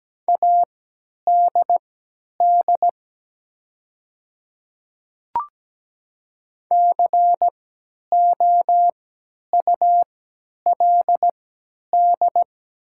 Morse code, heard as a keyed tone, about 17 wpm, spaced as for 10 wpm, 700 Hz. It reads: ADD COULD
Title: 3x Character Spacing - Top 200 Words - 17wpm